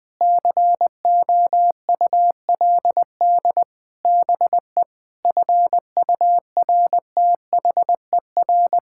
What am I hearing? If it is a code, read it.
Morse code, heard as a keyed tone, about 20 wpm, 700 Hz. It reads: COULD BE FURTHER
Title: Sets of 6 Words - 20wpm